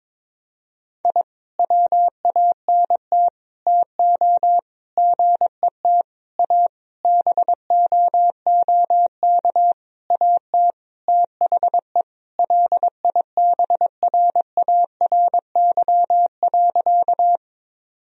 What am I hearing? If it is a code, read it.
Morse code, heard as a keyed tone, about 22 wpm, 700 Hz. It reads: I WANT TO GET A BOOK AT THE LIBRARY.
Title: Sentences from Top 1000 Words - Review Part 1 - 22wpm